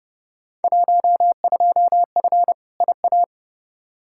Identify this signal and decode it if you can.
Morse code, heard as a keyed tone, about 31 wpm, 700 Hz. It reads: 12F IA
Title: Field Day Exchanges Encore - 30wpm